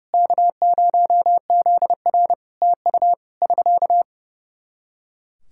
Morse code, heard as a keyed tone, about 31 wpm, 700 Hz. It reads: K0ZR TU <SK>
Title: Practice QSO - Exchange 4 - Ending - 30wpm